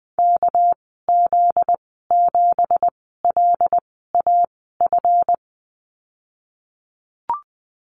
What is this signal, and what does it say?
Morse code, heard as a keyed tone, about 20 wpm, spaced as for 15 wpm, 700 Hz. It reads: KZ7LAF
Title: 245 Lesson 39g - Letter z - Call Signs 20wpm